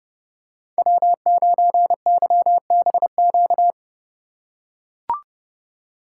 Morse code, heard as a keyed tone, about 32 wpm, 700 Hz. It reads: W9YBQ
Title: USA Call Signs - 5 Characters - 4x Character Spacing - 30wpm